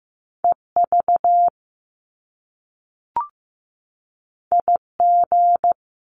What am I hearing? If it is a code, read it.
Morse code, heard as a keyed tone, about 15 wpm, 700 Hz. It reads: EV IG
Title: Top Two Letter Combinations in Words 15wpm